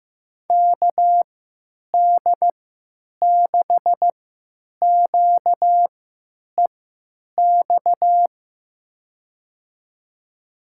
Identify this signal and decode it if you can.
Morse code, heard as a keyed tone, about 15 wpm, spaced as for 9 wpm, 700 Hz. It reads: KD6QEX
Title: USA Call Signs - 6 Characters - 3x Character Spacing - 15wpm